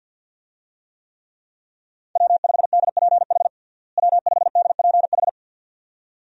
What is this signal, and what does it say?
Morse code, heard as a keyed tone, about 51 wpm, 700 Hz. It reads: W5DPH W5DPH
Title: USA Call Signs - 5 Characters - Repeated 2x - 50wpm